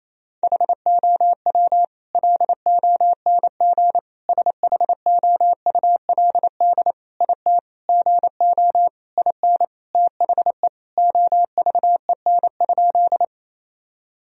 Morse code, heard as a keyed tone, about 28 wpm, 700 Hz. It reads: HOW LONG SHOULD IT GO IN THE OVEN?
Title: Sentences from Top 1000 Words - Review Part 4 - 28wpm